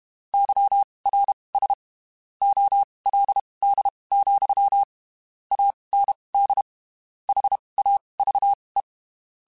Morse code, heard as a keyed tone, about 32 wpm, spaced as for 23 wpm, 800 Hz. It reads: YRS OLD, AND HAVE
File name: (spaced) / QSOs Spaced-mp3s / kc7sac @30 sp